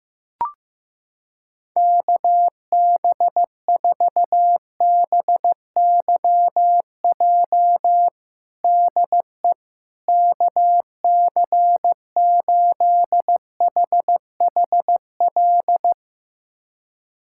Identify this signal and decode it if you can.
Morse code, heard as a keyed tone, about 15 wpm, 700 Hz. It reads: KB4BYJ DE KC8HHL
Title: USA Callsign DE Callsign 15wpm